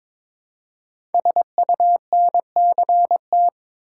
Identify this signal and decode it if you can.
Morse code, heard as a keyed tone, about 22 wpm, 700 Hz. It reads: SUNCT
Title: Code Groups - 5 Letter:Numbers - 22wpm